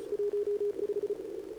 Morse code, heard as a keyed tone, about 35 wpm, 410 Hz, with radio noise. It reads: 15
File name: CQWWCW2025 / CQWWCW2025 80mRozjezd